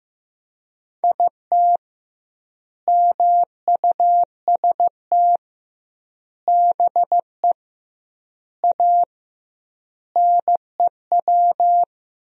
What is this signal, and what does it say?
Morse code, heard as a keyed tone, about 15 wpm, 700 Hz. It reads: IT MUST BE A NEW
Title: 2x Word-Spacing - Sentences from Top 500 Words - 15wpm